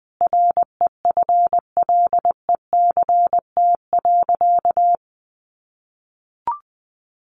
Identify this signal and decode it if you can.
Morse code, heard as a keyed tone, about 20 wpm, 700 Hz. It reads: REFLECT.
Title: Sentences from Top 1500 Words 20wpm